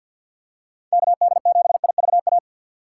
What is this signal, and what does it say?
Morse code, heard as a keyed tone, about 54 wpm, 700 Hz. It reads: KD7IVA
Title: USA Call Signs - 6 Characters - 6x Character Spacing - 50wpm